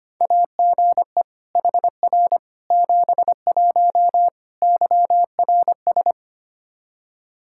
Morse code, heard as a keyed tone, about 25 wpm, 700 Hz. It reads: AGE HR 71 YRS
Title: Practice QSO - Exchange 3 - More Info - 25wpm